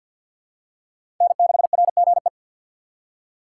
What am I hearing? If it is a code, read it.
Morse code, heard as a keyed tone, about 47 wpm, 700 Hz. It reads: N6RGE
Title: USA Call Signs - 5 Characters - 6x Character Spacing - 50wpm